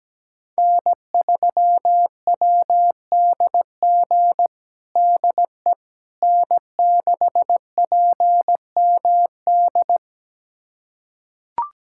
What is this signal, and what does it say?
Morse code, heard as a keyed tone, about 17 wpm, 700 Hz. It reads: N3WDG DE N6PMD
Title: USA Callsign DE Callsign - 5 Characters - 17wpm